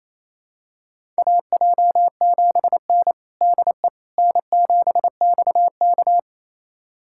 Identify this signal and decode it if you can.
Morse code, heard as a keyed tone, about 28 wpm, 700 Hz. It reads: AJ7N DE N7XK
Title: USA Callsign DE Callsign - 4 Characters - 28wpm